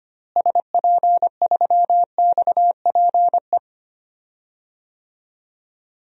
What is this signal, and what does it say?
Morse code, heard as a keyed tone, about 25 wpm, 700 Hz. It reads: SP3XPE
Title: DXCC-100 Call Signs - 6 Characters - Repeated 2x - 25wpm